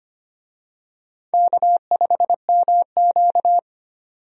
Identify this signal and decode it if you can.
Morse code, heard as a keyed tone, about 25 wpm, 700 Hz. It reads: K5MQ
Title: USA Call Signs - 4 Characters - 25wpm